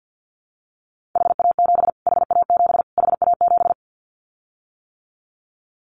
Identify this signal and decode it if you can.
Morse code, heard as a keyed tone, about 50 wpm, 700 Hz. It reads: HA7 HA7 HA7
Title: Callsign Prefixes for Top 25 DXCC - Repeated 3x - 50wpm